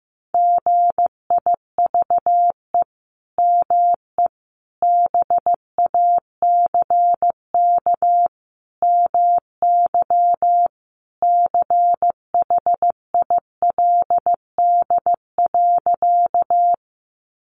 Morse code, heard as a keyed tone, about 15 wpm, 700 Hz. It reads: GIVE ME BACK MY CHILD.
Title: Sentences from Top 800 Words 15wpm